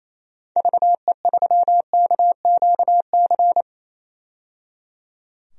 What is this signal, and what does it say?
Morse code, heard as a keyed tone, about 28 wpm, 700 Hz. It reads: VE3KQC
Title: DXCC-100 Call Signs - 6 Characters - 28wpm